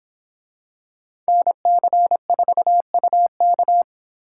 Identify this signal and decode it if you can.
Morse code, heard as a keyed tone, about 26 wpm, 700 Hz. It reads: NC4UK